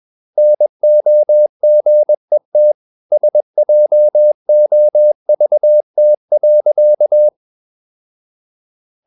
Morse code, heard as a keyed tone, about 21 wpm, 600 Hz. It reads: NOGET SJOVT.
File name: Ferie QRS 21wpm